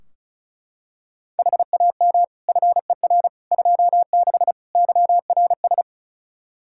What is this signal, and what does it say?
Morse code, heard as a keyed tone, about 36 wpm, 700 Hz. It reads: HAM FER 26 YRS
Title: Practice QSO - Exchange 3 - More Info - 35wpm